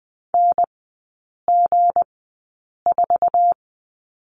Morse code, heard as a keyed tone, about 19 wpm, spaced as for 8 wpm, 690 Hz. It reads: NG4
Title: 159 Lesson 26e - Letter g - Three Characters 20wpm